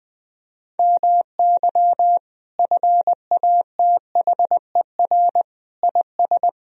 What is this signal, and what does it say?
Morse code, heard as a keyed tone, about 20 wpm, 700 Hz. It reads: MY FATHER IS
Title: Sentences from Top 1000 Words - Review Part 1 - 20wpm